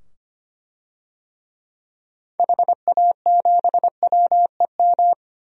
Morse code, heard as a keyed tone, about 25 wpm, 700 Hz. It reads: HA7WEM
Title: DXCC-100 Call Signs - 6 Characters - 25wpm